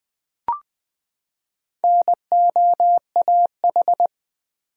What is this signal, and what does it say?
Morse code, heard as a keyed tone, about 20 wpm, 700 Hz. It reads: NOAH
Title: USA Common Names - Max 4 Letters - 20wpm